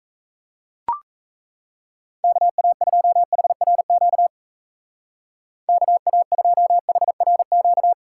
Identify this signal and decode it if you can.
Morse code, heard as a keyed tone, about 33 wpm, 700 Hz. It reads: KA2HRQ KA2HRQ
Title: Speed Racing - USA 2x3 Call Signs 28wpm